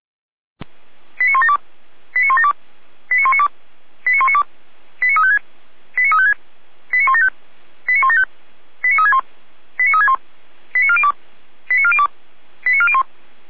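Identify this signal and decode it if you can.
Five-tone selective call (ZVEI), heard as quick runs of five tones, tones of about 70 ms. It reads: 89172 89172 89192 89192 89356 89356 89175 89175 89361 89361 89402 89402 89401
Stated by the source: ZVEI five-tone selective call received on an SDR.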